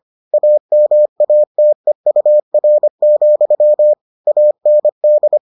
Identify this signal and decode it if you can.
Morse code, heard as a keyed tone, about 25 wpm, 600 Hz. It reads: AMATEUR, AND